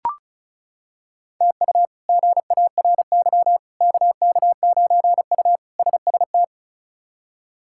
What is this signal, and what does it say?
Morse code, heard as a keyed tone, about 36 wpm, 700 Hz. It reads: TU GARY KK9U SST
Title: K1USN SST QSO Practice - 35wpm